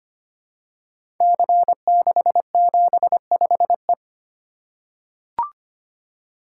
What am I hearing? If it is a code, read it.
Morse code, heard as a keyed tone, about 25 wpm, 700 Hz. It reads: C675E
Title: Code Groups - 5 Letter:Numbers - 25wpm